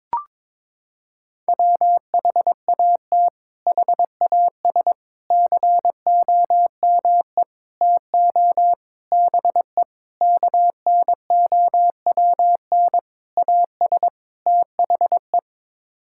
Morse code, heard as a keyed tone, about 22 wpm, 700 Hz. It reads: WHAT HAS COME TO BE KNOWN AS THE
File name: practice / 22wpm / Sets of 8 Words - 22wpm